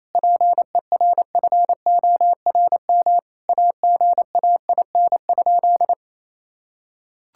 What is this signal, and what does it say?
Morse code, heard as a keyed tone, about 27 wpm, 700 Hz. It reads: PERFORM AGAIN?
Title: Sentences from Top 1000 Words - Review Part 5 - 28wpm